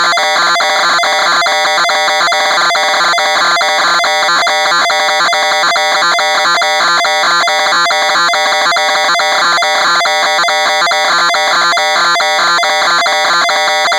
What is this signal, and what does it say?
Two-tone teleprinter signal, about 420 Hz shift.